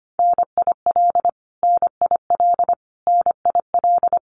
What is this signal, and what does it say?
Morse code, heard as a keyed tone, about 25 wpm, 700 Hz. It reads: NIL NIL NIL
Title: Abbreviations Repeated 3x - 25wpm